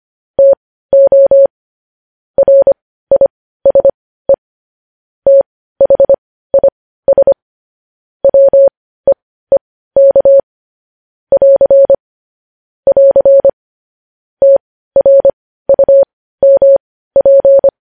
Morse code, heard as a keyed tone, about 25 wpm, spaced as for 15 wpm, 550 Hz. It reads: TO RISE THIS WEEK <AR> <AR> TRUMP
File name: podcast / News headlines-1x-25x15